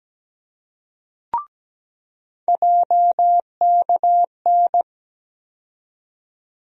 Morse code, heard as a keyed tone, about 17 wpm, 700 Hz. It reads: JKN
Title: USA Call Sign Postfixes - 3 Letters 17wpm